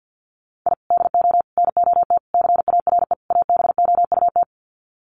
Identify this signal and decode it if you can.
Morse code, heard as a keyed tone, about 50 wpm, 700 Hz. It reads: I DO NOT CARE ABOUT
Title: Sets of 5 Words Encore2 50wpm